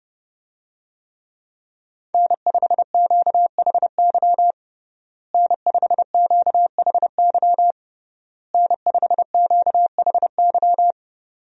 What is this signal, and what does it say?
Morse code, heard as a keyed tone, about 30 wpm, 700 Hz. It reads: N5QHY N5QHY N5QHY
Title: USA Call Signs - 5 Characters - Repeated 3x - 30wpm